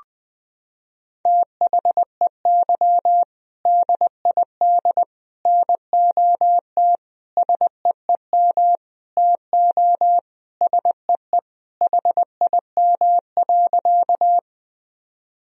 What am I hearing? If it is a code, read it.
Morse code, heard as a keyed tone, about 21 wpm, 700 Hz. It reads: THEY DID NOT SEEM TO SEE HIM.
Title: Sentences from Top 1000 Words - Review Part 1 - 20wpm